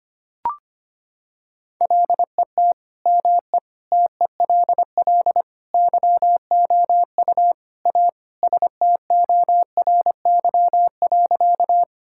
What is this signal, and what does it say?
Morse code, heard as a keyed tone, about 25 wpm, 700 Hz. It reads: LET ME TELL YOU A STORY.
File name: practice / 25wpm / Sentences with 4-Word Phrases - 25wpm